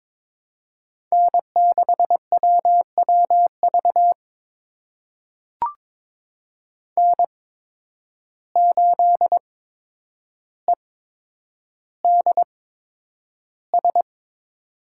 Morse code, heard as a keyed tone, about 21 wpm, spaced as for 9 wpm, 700 Hz. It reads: N6WWV N8EDS
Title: USA Call Signs - 8x Character Spacing - 22wpm